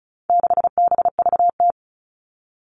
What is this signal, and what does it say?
Morse code, heard as a keyed tone, about 34 wpm, 700 Hz. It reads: 6BVT
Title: All Cut Numbers - 4-Digit 35wpm